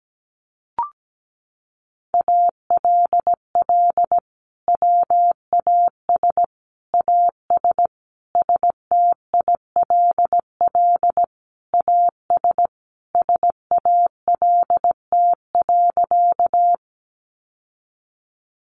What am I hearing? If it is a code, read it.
Morse code, heard as a keyed tone, about 17 wpm, 700 Hz. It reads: ALL WAS AS STILL AS SALT.
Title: Sentences from Top 800 Words 17wpm